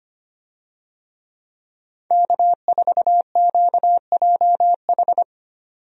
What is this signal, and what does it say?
Morse code, heard as a keyed tone, about 25 wpm, 700 Hz. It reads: K4QJH
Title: USA Call Signs - 5 Characters - 25wpm